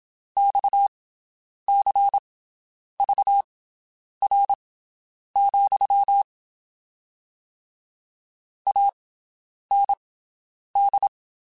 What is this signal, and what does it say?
Morse code, heard as a keyed tone, about 27 wpm, spaced as for 9 wpm, 800 Hz. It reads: XCVR, AND
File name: QSOs Spaced